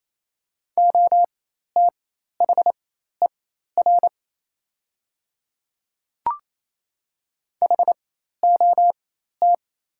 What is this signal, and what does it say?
Morse code, heard as a keyed tone, about 28 wpm, spaced as for 13 wpm, 700 Hz. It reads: OTHER HOT
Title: 4x Character Spacing - Top 100 Words - 28wpm